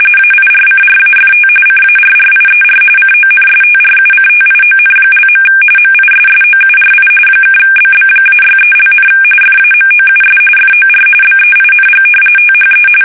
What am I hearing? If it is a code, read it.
RTTY, about 45.45 baud. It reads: AIRCRAFT CARRIER CONSTRUCTION). MISSOURI (BB-63), FAMOUS FOR BEING THE
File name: fsk850 / battleships